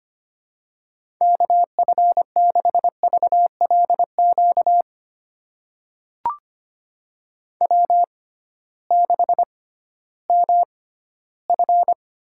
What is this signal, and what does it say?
Morse code, heard as a keyed tone, about 25 wpm, spaced as for 15 wpm, 700 Hz. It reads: KF6VLQ W6MF
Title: USA Call Signs - 6x Character Spacing - 25wpm